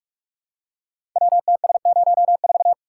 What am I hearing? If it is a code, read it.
Morse code, heard as a keyed tone, about 44 wpm, 700 Hz. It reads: WTS04
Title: Code Groups - 5 Letter:Numbers - 45wpm